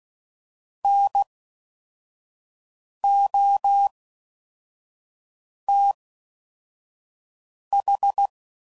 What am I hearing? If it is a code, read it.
Morse code, heard as a keyed tone, about 16 wpm, spaced as for 4 wpm, 800 Hz. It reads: NOTH